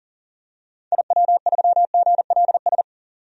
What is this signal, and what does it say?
Morse code, heard as a keyed tone, about 40 wpm, 700 Hz. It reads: IW3GLS
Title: DXCC-100 Call Signs - 6 Characters - 40wpm